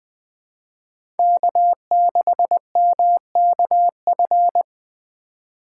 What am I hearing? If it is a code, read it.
Morse code, heard as a keyed tone, about 21 wpm, 700 Hz. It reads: K6MKF
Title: K1USN SST QSO Practice - 20wpm